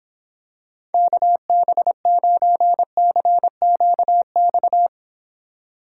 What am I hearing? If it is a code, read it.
Morse code, heard as a keyed tone, about 26 wpm, 700 Hz. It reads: KB9CQX